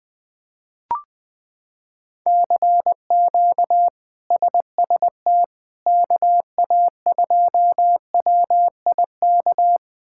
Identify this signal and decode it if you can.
Morse code, heard as a keyed tone, about 20 wpm, 700 Hz. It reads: CQ SST KA2WIK
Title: K1USN SST QSO Practice - 20wpm